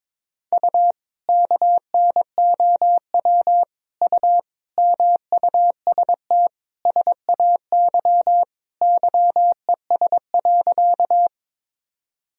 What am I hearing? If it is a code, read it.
Morse code, heard as a keyed tone, about 22 wpm, 700 Hz. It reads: U KNOW U MUST SAY YES.
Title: Sentences from Top 500 Words and Abbreviations 22wpm